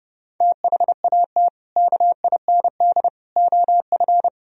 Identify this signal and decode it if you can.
Morse code, heard as a keyed tone, about 30 wpm, 700 Hz. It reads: THAT KIND OF
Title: Sentences from Top 1000 Words - Review Part 3 - 30wpm